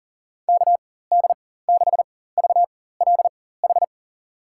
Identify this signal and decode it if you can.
Morse code, heard as a keyed tone, about 40 wpm, spaced as for 19 wpm, 700 Hz. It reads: KD6VLH
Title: USA Call Signs - 6 Characters - 4x Character Spacing - 40wpm